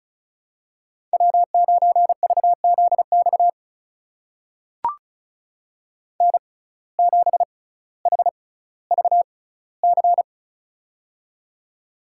Morse code, heard as a keyed tone, about 36 wpm, spaced as for 18 wpm, 700 Hz. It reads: W9VZX N7HVC
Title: USA Call Signs - 5 Characters - 6x Character Spacing - 35wpm